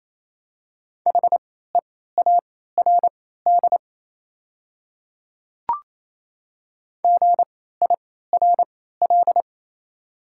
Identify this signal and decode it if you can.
Morse code, heard as a keyed tone, about 28 wpm, spaced as for 16 wpm, 690 Hz. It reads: HEARD GIRL
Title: 3x Character Spacing - Top 500 Words - 28wpm